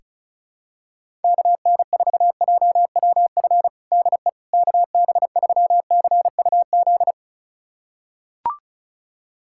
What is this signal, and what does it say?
Morse code, heard as a keyed tone, about 36 wpm, 700 Hz. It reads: KN4JWF DE KB3CUZ
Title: USA Callsign DE Callsign 35wpm